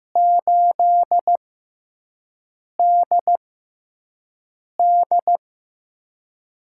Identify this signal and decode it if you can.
Morse code, heard as a keyed tone, about 15 wpm, spaced as for 5 wpm, 700 Hz. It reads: 8DD
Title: USA Call Signs - 6x Character Spacing - 15wpm